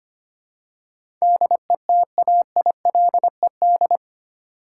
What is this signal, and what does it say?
Morse code, heard as a keyed tone, about 25 wpm, 700 Hz. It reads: DETAILED
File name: Eight Letter Words - 25wpm